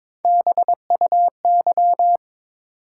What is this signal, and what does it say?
Morse code, heard as a keyed tone, about 22 wpm, 700 Hz. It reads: BUY
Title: Sentences from Top 700 Words - New Word Drill - 22wpm